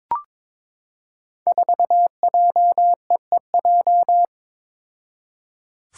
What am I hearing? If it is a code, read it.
Morse code, heard as a keyed tone, about 23 wpm, 700 Hz. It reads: 4JEEJ
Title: Five Challenging Letter-Number Combinations 22wpm